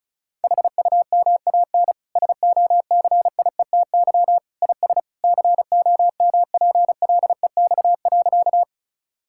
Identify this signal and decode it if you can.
Morse code, heard as a keyed tone, about 34 wpm, 700 Hz. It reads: HUMAN SOCIETY IS COMPLEX.
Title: Sentences from Top 1000 Words - Review Part 6 - 35wpm